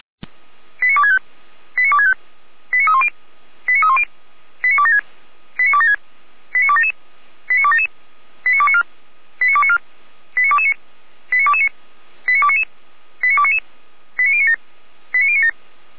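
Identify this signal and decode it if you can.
Five-tone selective call (ZVEI), heard as quick runs of five tones, tones of about 70 ms. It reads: 89265 89265 89310 89310 89276 89276 89277 89277 89294 89294 89229 89229 89299 89299 89097 89097
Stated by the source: ZVEI five-tone selective call received with an SDR.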